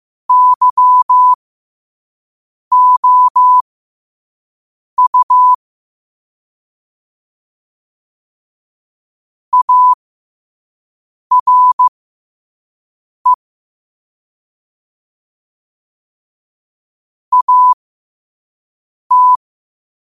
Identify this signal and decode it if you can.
Morse code, heard as a keyed tone, about 15 wpm, spaced as for 5 wpm, 1000 Hz. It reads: YOU ARE AT
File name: qotdfiles / 5wpm